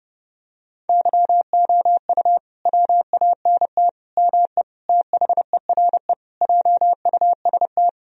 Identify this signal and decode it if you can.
Morse code, heard as a keyed tone, about 32 wpm, 700 Hz. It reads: YOU WANT ME THERE JUST
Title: Sentences from Top 1000 Words - Review Part 2 - 30wpm